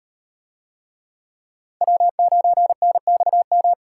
Morse code, heard as a keyed tone, about 36 wpm, 700 Hz. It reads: W9NXM